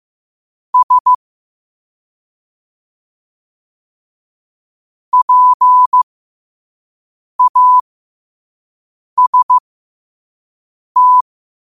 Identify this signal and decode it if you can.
Morse code, heard as a keyed tone, about 15 wpm, spaced as for 5 wpm, 1000 Hz. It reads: S PAST
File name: qotdfiles / 5wpm